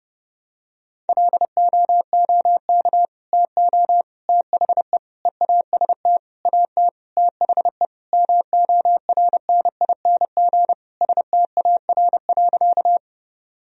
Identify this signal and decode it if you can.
Morse code, heard as a keyed tone, about 32 wpm, 700 Hz. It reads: LOOK TO THE EAST AT THE MORNING STAR.
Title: Sentences from Top 1000 Words - Review Part 3 - 30wpm